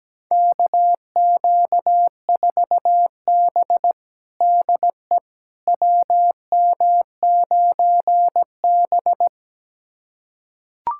Morse code, heard as a keyed tone, about 17 wpm, 700 Hz. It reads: KQ4B DE WM9B
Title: USA Callsign DE Callsign - 4 Characters - 17wpm